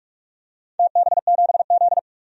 Morse code, heard as a keyed tone, about 46 wpm, 700 Hz. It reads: TB7Z
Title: DXCC-100 Call Signs - 4 Characters - 45wpm